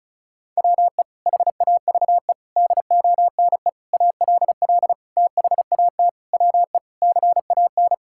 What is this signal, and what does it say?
Morse code, heard as a keyed tone, about 34 wpm, 700 Hz. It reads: WE HAVE DONE ALL THAT WE CAN
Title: Sentences from Top 1000 Words - Review Part 5 - 35wpm